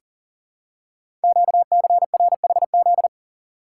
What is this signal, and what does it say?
Morse code, heard as a keyed tone, about 39 wpm, 700 Hz. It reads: QCRHZ
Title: Code Groups - 5 Letters - 40wpm